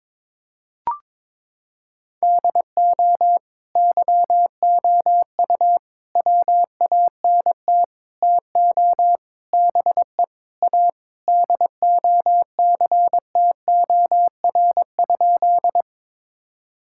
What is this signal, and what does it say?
Morse code, heard as a keyed tone, about 22 wpm, 700 Hz. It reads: DO YOU WANT TO BE A DOCTOR?
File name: practice / 22wpm / Sentences with 4-Word Phrases - 22wpm